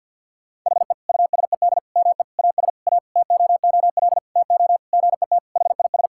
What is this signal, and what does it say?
Morse code, heard as a keyed tone, about 50 wpm, 700 Hz. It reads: HE USED ME AS A TOOL TO GET HIS
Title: Sentences from Top 1000 Words - Review Part 6 - 50wpm